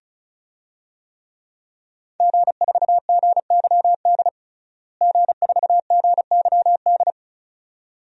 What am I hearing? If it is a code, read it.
Morse code, heard as a keyed tone, about 36 wpm, 700 Hz. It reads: G4GYD G4GYD
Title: DXCC-100 Call Signs - 5 Characters - Repeated 2x - 35wpm